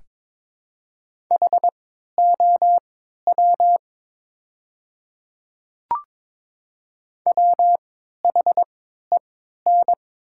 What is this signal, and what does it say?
Morse code, heard as a keyed tone, about 22 wpm, spaced as for 12 wpm, 700 Hz. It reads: HOW WHEN
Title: 3x Character Spacing - Top 100 Words - 22wpm